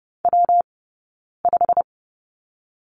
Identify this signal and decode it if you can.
Morse code, heard as a keyed tone, about 30 wpm, 700 Hz. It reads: W 5
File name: Lesson 19d - Letter w - Two Characters 30wpm